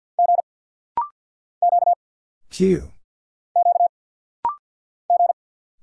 Morse code, heard as a keyed tone, about 50 wpm, 700 Hz. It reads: G Q Q G
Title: Single Letters - Rapid-Fire 50wpm